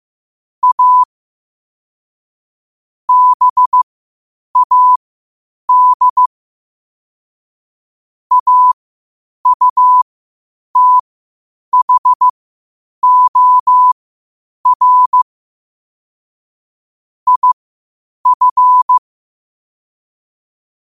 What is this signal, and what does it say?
Morse code, heard as a keyed tone, about 15 wpm, spaced as for 8 wpm, 1000 Hz. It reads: A BAD AUTHOR IF
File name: qotdfiles / 8wpm